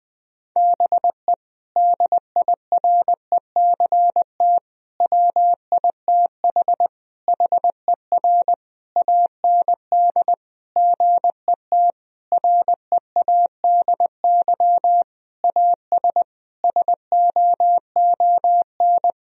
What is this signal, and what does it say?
Morse code, heard as a keyed tone, about 20 wpm, 700 Hz. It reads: BE DIRECT WITH HER AND GET READY AS SOON
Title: Sentences from Top 1000 Words - Review Part 2 - 20wpm